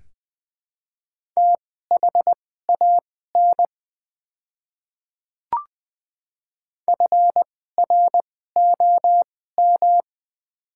Morse code, heard as a keyed tone, about 20 wpm, spaced as for 14 wpm, 700 Hz. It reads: THAN FROM
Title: 2x Character Spacing - Top 100 Words - 20wpm